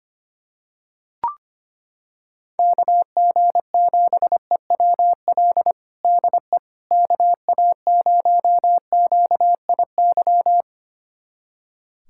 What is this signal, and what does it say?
Morse code, heard as a keyed tone, about 25 wpm, 700 Hz. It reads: KG7EWL DE KA0QIY
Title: USA Callsign DE Callsign - 6 Characters - 25wpm